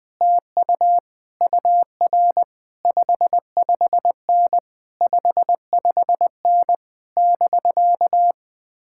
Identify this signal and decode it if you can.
Morse code, heard as a keyed tone, about 20 wpm, 700 Hz. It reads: TU UR 55N 55N <BK>
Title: Speed-Racing Common QSO Phrases - 17wpm